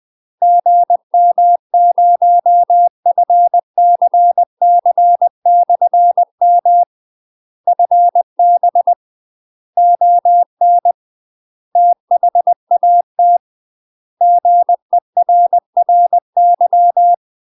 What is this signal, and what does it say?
Morse code, heard as a keyed tone, about 20 wpm, 700 Hz. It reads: GM0FCC/M FB ON THAT GERRY
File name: UK MASTER QSOs TRAINING / 20WPM / UK MASTER QSOs 0013